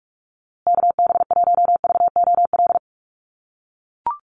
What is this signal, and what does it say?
Morse code, heard as a keyed tone, about 44 wpm, 700 Hz. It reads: KB1VOL
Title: USA Call Signs Repeated 3x 45wpm